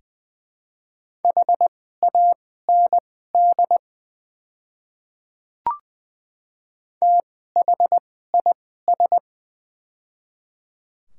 Morse code, heard as a keyed tone, about 20 wpm, spaced as for 15 wpm, 700 Hz. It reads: HAND THIS
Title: 2x Character Spacing - Top 200 Words - 20wpm